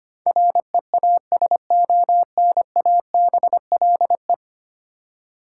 Morse code, heard as a keyed tone, about 25 wpm, 700 Hz. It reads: REASONABLE
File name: Ten Letter Words - 25wpm